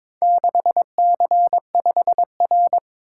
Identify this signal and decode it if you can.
Morse code, heard as a keyed tone, about 22 wpm, 700 Hz. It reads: 6C5R
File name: practice / 22wpm / Four Letter-Number Combinations - 22wpm